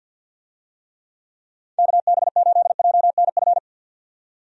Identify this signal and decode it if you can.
Morse code, heard as a keyed tone, about 48 wpm, 700 Hz. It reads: KB8JNF
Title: USA Call Signs - 6 Characters - 50wpm